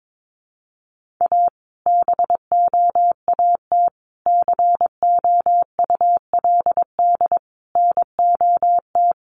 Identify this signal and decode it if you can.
Morse code, heard as a keyed tone, about 22 wpm, 700 Hz. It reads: A BOAT COULD NOT
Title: Sentences from Top 1400 Words 22wpm